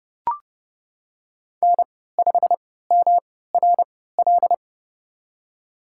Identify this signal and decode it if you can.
Morse code, heard as a keyed tone, about 31 wpm, spaced as for 17 wpm, 700 Hz. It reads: N5MRL
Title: USA Call Signs - 5 Characters - 3x Character Spacing - 30wpm